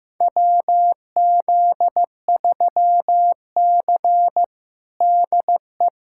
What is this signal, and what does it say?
Morse code, heard as a keyed tone, about 15 wpm, 700 Hz. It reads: WZ3C DE
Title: USA Callsign DE Callsign - 4 Characters - 15wpm